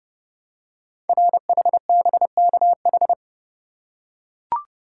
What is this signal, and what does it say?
Morse code, heard as a keyed tone, about 29 wpm, 700 Hz. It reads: RHBKH
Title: Code Groups - 5 Letters - 30wpm